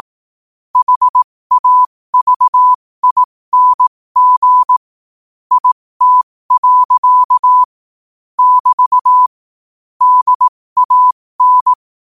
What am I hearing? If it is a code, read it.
Morse code, heard as a keyed tone, about 18 wpm, spaced as for 15 wpm, 1000 Hz. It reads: HAVING IT. <BT> DAN
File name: qotdfiles / 15wpm